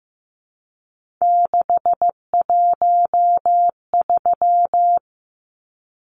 Morse code, heard as a keyed tone, about 15 wpm, 700 Hz. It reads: 613
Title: Numbers - 3-Digit 15wpm